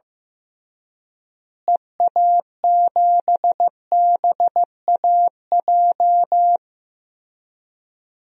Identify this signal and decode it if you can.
Morse code, heard as a keyed tone, about 15 wpm, 700 Hz. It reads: EA7BAJ
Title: DXCC-100 Call Signs - 6 Characters - Repeated 3x - 15wpm